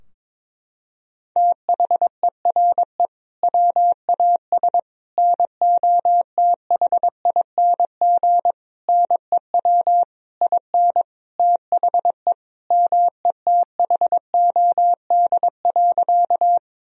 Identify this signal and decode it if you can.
Morse code, heard as a keyed tone, about 22 wpm, 700 Hz. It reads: THERE WAS NOTHING NEW IN THE METHOD.
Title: Sentences from Top 1000 Words - Review Part 3 - 22wpm